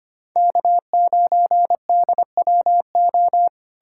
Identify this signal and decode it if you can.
Morse code, heard as a keyed tone, about 33 wpm, 700 Hz. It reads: K9DWO